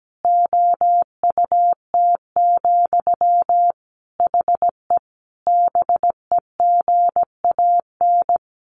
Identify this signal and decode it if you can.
Morse code, heard as a keyed tone, about 17 wpm, 700 Hz. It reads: OUT, HE BEGAN
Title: Sentences from Top 1500 Words 17wpm